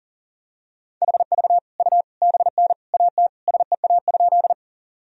Morse code, heard as a keyed tone, about 40 wpm, 700 Hz. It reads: HV U BN AT SEA?
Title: Sentences from Top 300 Words and Abbreviations 40wpm